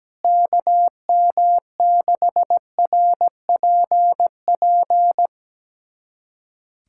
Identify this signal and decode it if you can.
Morse code, heard as a keyed tone, about 17 wpm, 700 Hz. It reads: KM6RPP
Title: USA Callsign DE Callsign - 6 Characters - 17wpm